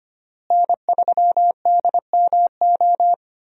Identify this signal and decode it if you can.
Morse code, heard as a keyed tone, about 25 wpm, 700 Hz. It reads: N3DMO
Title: USA Call Signs - 5 Characters - Repeated 3x - 25wpm